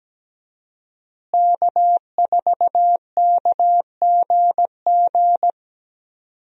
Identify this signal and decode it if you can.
Morse code, heard as a keyed tone, about 17 wpm, 700 Hz. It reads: K4KGG
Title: USA Call Signs - 5 Characters - Repeated 2x - 17wpm